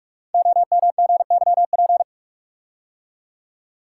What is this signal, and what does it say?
Morse code, heard as a keyed tone, about 44 wpm, 700 Hz. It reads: OMGYP
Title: Code Groups - 5 Letters - 45wpm